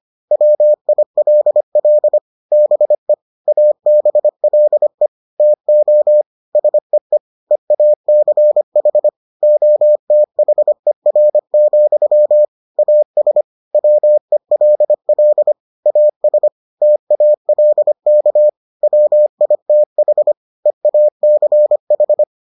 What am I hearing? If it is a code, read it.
Morse code, heard as a keyed tone, about 25 wpm, 600 Hz. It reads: WILL BE ABLE TO SEE EACH OTHER, AS WELL AS TALK WITH EACH